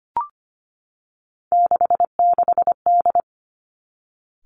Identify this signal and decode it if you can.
Morse code, heard as a keyed tone, about 25 wpm, 700 Hz. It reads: 66D
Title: Three Challenging Letter-Number Combinations 25wpm